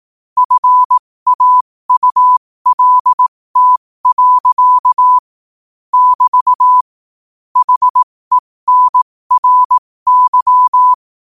Morse code, heard as a keyed tone, about 18 wpm, spaced as for 15 wpm, 1000 Hz. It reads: FAULT. <BT> HENRY
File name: qotdfiles / 15wpm